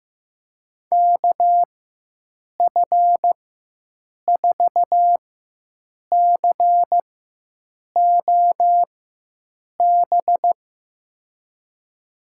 Morse code, heard as a keyed tone, about 15 wpm, spaced as for 7 wpm, 700 Hz. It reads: KF4COB
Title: USA Call Signs - 6 Characters - 4x Character Spacing - 15wpm